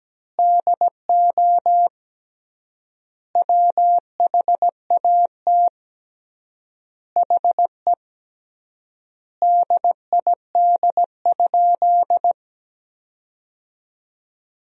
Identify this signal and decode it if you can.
Morse code, heard as a keyed tone, about 17 wpm, 700 Hz. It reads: DO WHAT HE DID?
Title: 3x Word-Spacing - Sentences from Top 100 Words - 17wpm